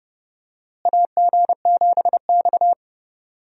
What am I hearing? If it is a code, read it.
Morse code, heard as a keyed tone, about 31 wpm, 700 Hz. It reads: AG7X
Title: USA Call Signs - 4 Characters - Repeated 3x - 30wpm